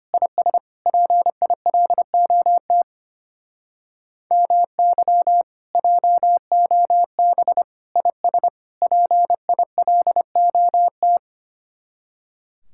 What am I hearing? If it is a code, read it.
Morse code, heard as a keyed tone, about 25 wpm, 700 Hz. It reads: IS PILOT MY JOB IS PILOT
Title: Speed-Racing Common QSO Phrases Encore - 25wpm